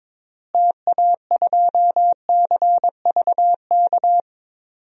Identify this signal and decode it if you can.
Morse code, heard as a keyed tone, about 22 wpm, 700 Hz. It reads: TA2CVK
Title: DXCC-100 Call Signs - 6 Characters - Repeated 3x - 22wpm